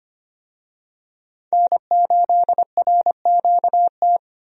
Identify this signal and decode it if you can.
Morse code, heard as a keyed tone, about 25 wpm, 700 Hz. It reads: N8RQT
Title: USA Call Signs - 5 Characters - 25wpm